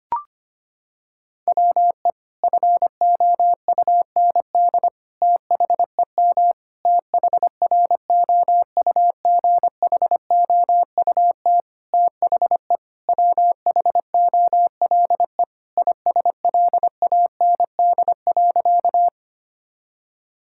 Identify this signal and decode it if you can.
Morse code, heard as a keyed tone, about 25 wpm, 700 Hz. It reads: WE FOUND THEM THROUGHOUT THE WHOLE ISLAND.
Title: Sentences from Top 1000 Words - Review Part 5 - 25wpm